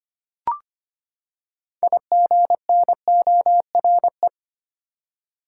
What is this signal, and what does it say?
Morse code, heard as a keyed tone, about 25 wpm, 700 Hz. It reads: IGNORE
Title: Six Letter Words - 25wpm